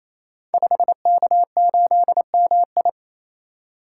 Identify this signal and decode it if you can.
Morse code, heard as a keyed tone, about 27 wpm, 700 Hz. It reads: 5K8MI
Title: Code Groups - 5 Letter:Numbers - 28wpm